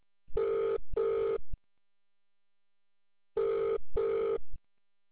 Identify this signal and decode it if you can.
Ringback tone (UK), 400 + 450 Hz.